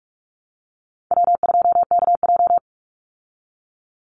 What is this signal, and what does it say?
Morse code, heard as a keyed tone, about 44 wpm, 700 Hz. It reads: W2KJ
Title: Easy Call Signs Repeated 3x - 45wpm